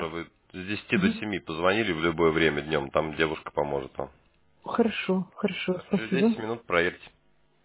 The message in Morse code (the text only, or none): none